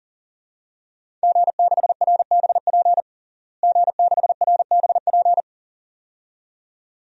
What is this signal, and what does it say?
Morse code, heard as a keyed tone, about 41 wpm, 700 Hz. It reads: G6RBP G6RBP
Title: DXCC-100 Call Signs - 5 Characters - Repeated 2x - 40wpm